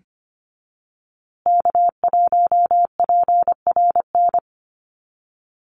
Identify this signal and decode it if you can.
Morse code, heard as a keyed tone, about 25 wpm, 700 Hz. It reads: K1PRN
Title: USA Call Signs Repeated 2x 25wpm